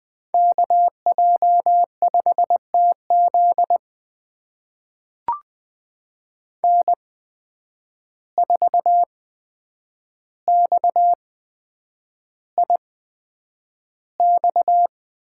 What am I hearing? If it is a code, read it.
Morse code, heard as a keyed tone, about 20 wpm, spaced as for 9 wpm, 700 Hz. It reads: KJ5TZ N4XIX
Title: USA Call Signs - 5 Characters - 8x Character Spacing - 20wpm